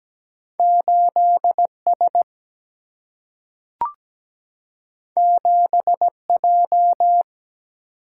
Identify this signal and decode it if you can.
Morse code, heard as a keyed tone, about 17 wpm, 700 Hz. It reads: 8 S 7 J
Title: Single-Letter Single-Number 17wpm